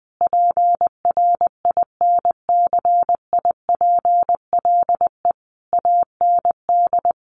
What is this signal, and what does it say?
Morse code, heard as a keyed tone, about 20 wpm, 700 Hz. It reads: PRINCIPLE AND
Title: Sentences from Top 1400 Words 20wpm